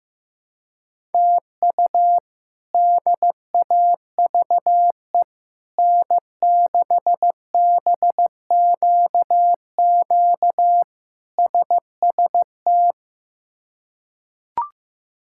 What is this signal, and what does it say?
Morse code, heard as a keyed tone, about 16 wpm, 700 Hz. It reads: TU DAVE N6BQQ SST
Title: K1USN SST QSO Practice - 15wpm